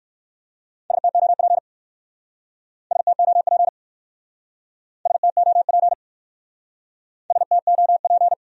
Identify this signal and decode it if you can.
Morse code, heard as a keyed tone, about 62 wpm, 700 Hz. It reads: STOP STOP STOP STOP